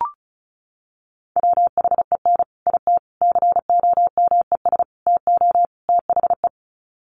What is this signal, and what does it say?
Morse code, heard as a keyed tone, about 34 wpm, 700 Hz. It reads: WHEN IT COMES TO THE
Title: Sets of 5 Words 35wpm